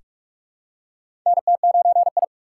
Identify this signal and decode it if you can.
Morse code, heard as a keyed tone, about 42 wpm, 700 Hz. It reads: NT9I
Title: USA Call Signs - 3x Character Spacing - 45wpm